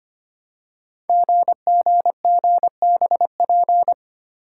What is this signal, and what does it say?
Morse code, heard as a keyed tone, about 25 wpm, 700 Hz. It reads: GGGBP
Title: Code Groups - 5 Letters - 25wpm